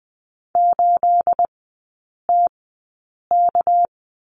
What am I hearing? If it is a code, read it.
Morse code, heard as a keyed tone, about 20 wpm, spaced as for 8 wpm, 700 Hz. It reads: 8TK
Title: 212 Lesson 34e - Number 8 - Three Characters 20wpm